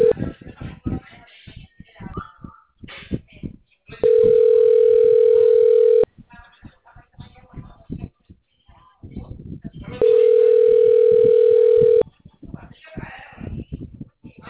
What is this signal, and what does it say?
Ringback tone (North America), 440 + 480 Hz, with speech.
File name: out- -2014- - - .14197